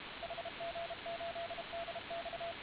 Morse code, heard as a keyed tone, about 34 wpm, 680 Hz, with radio noise. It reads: HG8DX